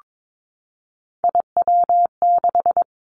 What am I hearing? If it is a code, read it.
Morse code, heard as a keyed tone, about 22 wpm, 700 Hz. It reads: IW6
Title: Three Challenging Letter-Number Combinations 22wpm